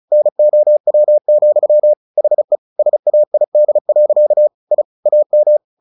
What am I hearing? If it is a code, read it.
Morse code, heard as a keyed tone, about 34 wpm, 600 Hz. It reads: NOW, HE SAID. I AM
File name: the old man and the sea 2 35wpm